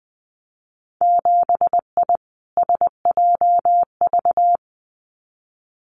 Morse code, heard as a keyed tone, about 20 wpm, 700 Hz. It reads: 7I SJV
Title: Winter Field Day Exchanges Encore 20wpm